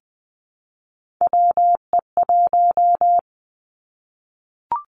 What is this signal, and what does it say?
Morse code, heard as a keyed tone, about 20 wpm, 700 Hz. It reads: WE1
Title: Three Challenging Letter-Number Combinations 20wpm